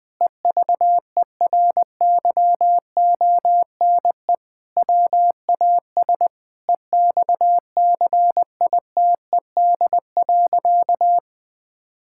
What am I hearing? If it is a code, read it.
Morse code, heard as a keyed tone, about 20 wpm, 700 Hz. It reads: EVERYONE WAS EXCITED.
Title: Sentences with 5-Word Phrases - 20wpm